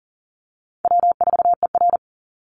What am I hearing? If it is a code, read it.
Morse code, heard as a keyed tone, about 40 wpm, 700 Hz. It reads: W4ER
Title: Easy Call Signs - 40wpm